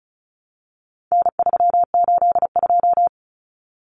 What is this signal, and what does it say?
Morse code, heard as a keyed tone, about 34 wpm, 700 Hz. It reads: N382
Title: Common Cut Numbers - 4-Digit 35wpm